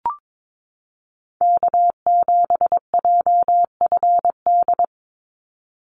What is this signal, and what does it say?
Morse code, heard as a keyed tone, about 22 wpm, 700 Hz. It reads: K7JFD
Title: NA SOTA Chaser Callsigns 22wpm